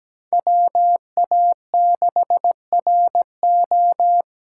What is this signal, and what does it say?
Morse code, heard as a keyed tone, about 17 wpm, 700 Hz. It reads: WA6RO
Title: USA Call Signs - 5 Characters - 6x Character Spacing - 17wpm